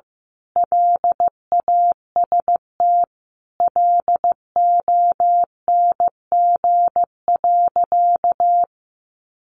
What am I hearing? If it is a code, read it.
Morse code, heard as a keyed tone, about 15 wpm, 700 Hz. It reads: LAST LONG.
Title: Sentences from Top 1300 Words 15wpm